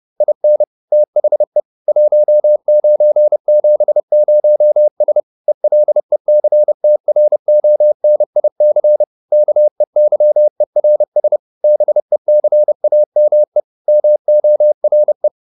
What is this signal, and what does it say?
Morse code, heard as a keyed tone, about 30 wpm, 600 Hz. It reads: IN THE 1970S ELECTRONIC KEYERS BECAME MORE